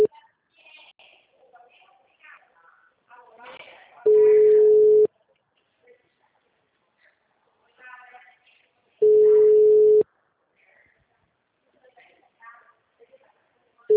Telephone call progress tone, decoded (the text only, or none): ringback tone